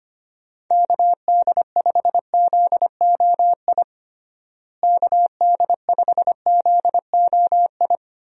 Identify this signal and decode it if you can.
Morse code, heard as a keyed tone, about 25 wpm, 700 Hz. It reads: KD5ZOI KD5ZOI
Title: USA Call Signs - 6 Characters - Repeated 3x - 25wpm